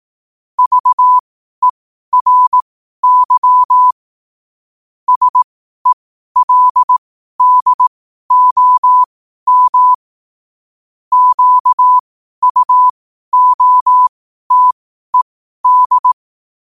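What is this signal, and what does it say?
Morse code, heard as a keyed tone, about 18 wpm, spaced as for 12 wpm, 1000 Hz. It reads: VERY SELDOM QUOTED
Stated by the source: Morse code sent at 12 words per minute.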